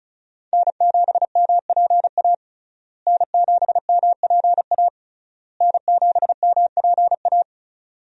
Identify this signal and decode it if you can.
Morse code, heard as a keyed tone, about 34 wpm, 700 Hz. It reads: N7MPA N7MPA N7MPA
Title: USA Call Signs - 5 Characters - Repeated 3x - 35wpm